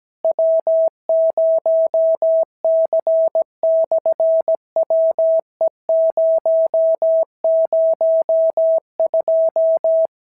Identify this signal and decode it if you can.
Morse code, heard as a keyed tone, about 17 wpm, 650 Hz. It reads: W0C/WE002
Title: SOTA QSO Practice 17wpm